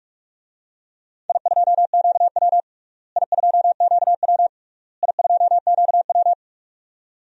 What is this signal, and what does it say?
Morse code, heard as a keyed tone, about 46 wpm, 700 Hz. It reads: I2QW I2QW I2QW
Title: DXCC-100 Call Signs - 4 Characters - Repeated 3x - 45wpm